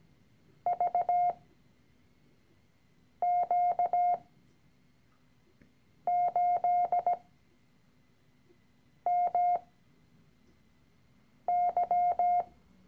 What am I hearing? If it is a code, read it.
Morse code, heard as a keyed tone, about 17 wpm, spaced as for 4 wpm, 710 Hz. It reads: VQ8MY
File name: CW-Online-Kurs / Anfänger / Lektion 013